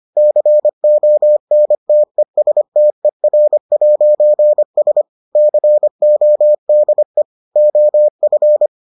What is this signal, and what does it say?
Morse code, heard as a keyed tone, about 25 wpm, 600 Hz. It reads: CONTESTER'S CODE OF